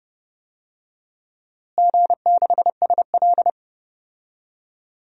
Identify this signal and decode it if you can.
Morse code, heard as a keyed tone, about 30 wpm, 700 Hz. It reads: G6SL
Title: DXCC-100 Call Signs - 4 Characters - Repeated 2x - 30wpm